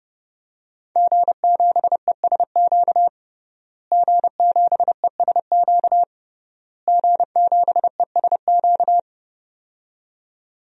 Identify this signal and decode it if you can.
Morse code, heard as a keyed tone, about 32 wpm, 700 Hz. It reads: G7ESQ G7ESQ G7ESQ
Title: DXCC-100 Call Signs - 5 Characters - Repeated 3x - 30wpm